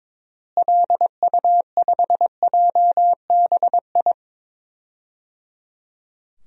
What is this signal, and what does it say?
Morse code, heard as a keyed tone, about 22 wpm, 700 Hz. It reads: LU5JBI
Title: DXCC-100 Call Signs - 6 Characters - Repeated 3x - 22wpm